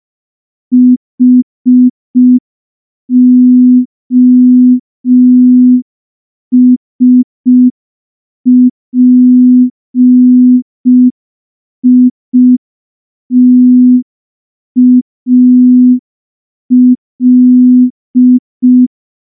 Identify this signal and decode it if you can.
Morse code, heard as a keyed tone, about 5 wpm, 250 Hz. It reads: HOSPITAL
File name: morse / morse-training-05WPM